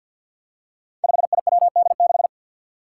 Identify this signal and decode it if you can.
Morse code, heard as a keyed tone, about 52 wpm, 700 Hz. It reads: 5IWD6
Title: Five Challenging Letter-Number Combinations 50wpm